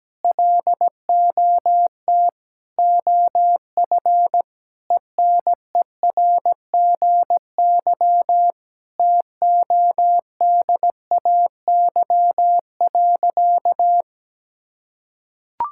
Morse code, heard as a keyed tone, about 17 wpm, 700 Hz. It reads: LOT OF ENERGY TODAY.
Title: Sentences from Top 1000 Words - Review Part 6 - 17wpm